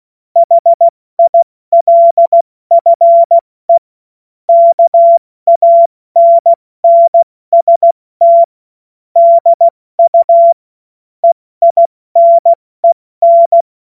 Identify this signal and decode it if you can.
Morse code, heard as a keyed tone, about 16 wpm, 680 Hz. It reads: HILFE KANNST DU EINEN
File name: Morsekurs / Witze / Witz 31.Oktober 2023 70 F80